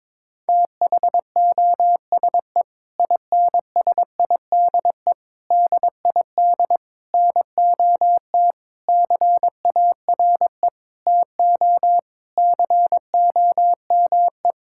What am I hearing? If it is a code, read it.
Morse code, heard as a keyed tone, about 22 wpm, 700 Hz. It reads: THOSE INSIDE DID NOT CARE TO COME
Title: Sentences from Top 1000 Words - Review Part 6 - 22wpm